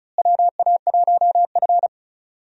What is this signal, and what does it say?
Morse code, heard as a keyed tone, about 34 wpm, 700 Hz. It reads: WA1F
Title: USA Call Signs - 4 Characters - Repeated 2x - 35wpm